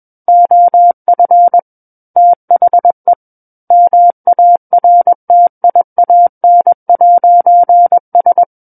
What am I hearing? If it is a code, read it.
Morse code, heard as a keyed tone, about 21 wpm, 700 Hz. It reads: OF THE MARTIAN'S
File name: Book One / 12-Chapter 12 - 21 WPM